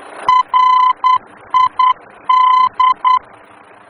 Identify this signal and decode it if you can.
Morse code, heard as a keyed tone, about 10 wpm, 1020 Hz. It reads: RID
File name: germany / mx / RID-D-VOR